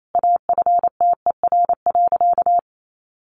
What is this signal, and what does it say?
Morse code, heard as a keyed tone, about 28 wpm, 700 Hz. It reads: AFTER.
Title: Sentences from Top 500 Words 28wpm